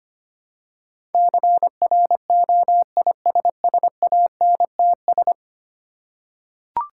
Morse code, heard as a keyed tone, about 25 wpm, 700 Hz. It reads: CROISSANTS
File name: practice / 25wpm / Difficult Words with Hints - 25wpm